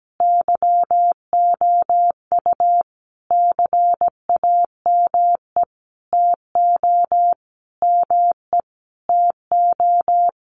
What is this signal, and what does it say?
Morse code, heard as a keyed tone, about 17 wpm, 700 Hz. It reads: YOU CAME TO ME TO
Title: Sentences from Top 200 Words 17wpm